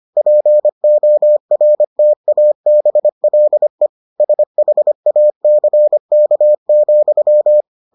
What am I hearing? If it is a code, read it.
Morse code, heard as a keyed tone, about 25 wpm, 600 Hz. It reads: PORTABLE SHACK,